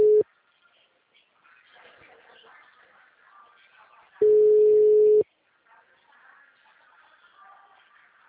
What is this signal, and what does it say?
Ringback tone (Europe, 425 Hz).
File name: out- -2015- - - .480